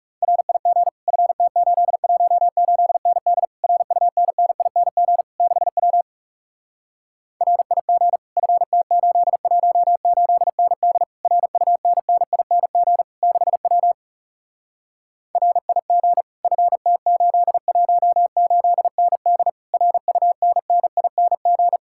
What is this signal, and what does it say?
Morse code, heard as a keyed tone, about 37 wpm, 700 Hz. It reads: RIG FT818ND RUNNING 6W RIG FT818ND RUNNING 6W RIG FT818ND RUNNING
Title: Speed-Racing Common QSO Phrases - 30wpm